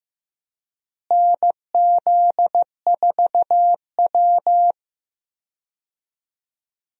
Morse code, heard as a keyed tone, about 15 wpm, 700 Hz. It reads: NZ4W
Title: USA Call Signs - 4 Characters - Repeated 3x - 15wpm